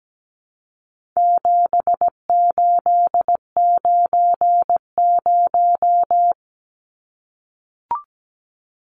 Morse code, heard as a keyed tone, about 17 wpm, 700 Hz. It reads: 7890
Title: Numbers - 4-Digit 17wpm